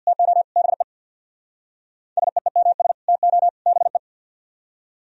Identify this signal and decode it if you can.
Morse code, heard as a keyed tone, about 60 wpm, 700 Hz. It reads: TO BE SEEMS TO BE
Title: Speed-Racing - Sets of 3 Words 50wpm